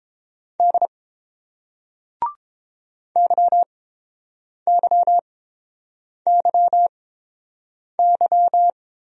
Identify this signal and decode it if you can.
Morse code, heard as a keyed tone, about 30 wpm, spaced as for 7 wpm, 700 Hz. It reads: D YYYY